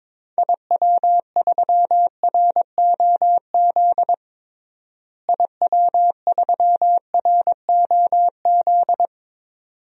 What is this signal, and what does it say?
Morse code, heard as a keyed tone, about 22 wpm, 700 Hz. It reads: IW3ROZ IW3ROZ
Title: DXCC-100 Call Signs - 6 Characters - Repeated 2x - 22wpm